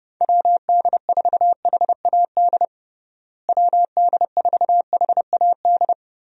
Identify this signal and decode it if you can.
Morse code, heard as a keyed tone, about 30 wpm, 700 Hz. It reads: WD4HAD WD4HAD
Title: USA Call Signs - 6 Characters - Repeated 3x - 30wpm